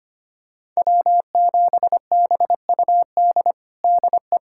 Morse code, heard as a keyed tone, about 25 wpm, 700 Hz. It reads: W7BUD DE
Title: USA Callsign DE Callsign - 5 Characters - 25wpm